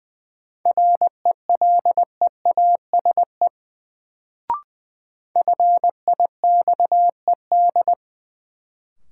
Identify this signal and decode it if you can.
Morse code, heard as a keyed tone, about 20 wpm, 700 Hz. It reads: RELEASE FIXED
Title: Sentences from Top 1100 Words - New Word Drill Alt - 20wpm